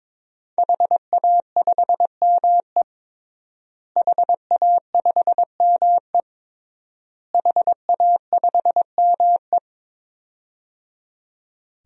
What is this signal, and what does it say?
Morse code, heard as a keyed tone, about 22 wpm, 700 Hz. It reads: HA5ME HA5ME HA5ME
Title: DXCC-100 Call Signs - 5 Characters - Repeated 3x - 22wpm